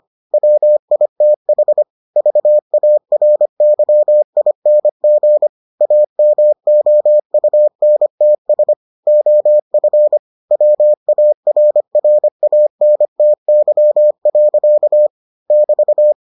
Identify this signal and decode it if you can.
Morse code, heard as a keyed tone, about 25 wpm, 600 Hz. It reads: WITH VARYING AMOUNTS OF WARRANTY. <BT>